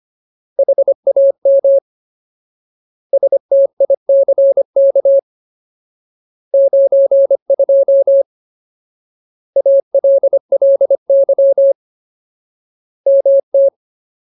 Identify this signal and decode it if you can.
Morse code, heard as a keyed tone, about 25 wpm, 550 Hz. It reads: HAM STICK 92 ALLY MT
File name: Ultimate Word List / 25WPM Ultimate